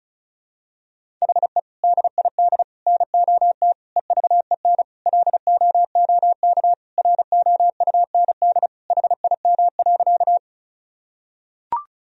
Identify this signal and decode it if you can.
Morse code, heard as a keyed tone, about 36 wpm, 700 Hz. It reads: HE DID NOT EVEN LOOK ROUND HIM.
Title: Sentences from Top 1000 Words - Review Part 1 - 35wpm